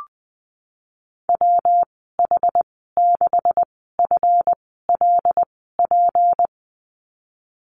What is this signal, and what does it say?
Morse code, heard as a keyed tone, about 20 wpm, spaced as for 15 wpm, 700 Hz. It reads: WH6FLP
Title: 231 Lesson 37g - Letter x - Call Signs 20wpm